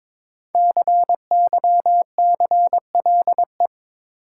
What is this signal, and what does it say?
Morse code, heard as a keyed tone, about 21 wpm, 700 Hz. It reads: CYCLE
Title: Sentences from Top 900 Words - New Word Drill - 22wpm